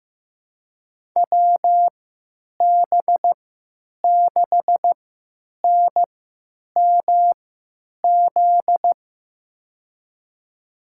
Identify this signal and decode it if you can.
Morse code, heard as a keyed tone, about 15 wpm, spaced as for 9 wpm, 700 Hz. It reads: WB6NMZ
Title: USA Call Signs - 6 Characters - 3x Character Spacing - 15wpm